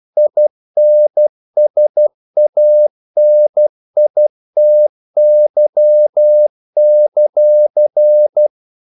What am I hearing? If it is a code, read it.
Morse code, heard as a keyed tone, about 12 wpm, 600 Hz. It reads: INSANITY;
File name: CW / Practice / EnglishQuotations0112